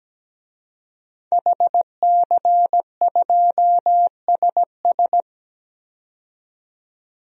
Morse code, heard as a keyed tone, about 17 wpm, 700 Hz. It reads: HC2SS
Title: DXCC-100 Call Signs - 5 Characters - 17wpm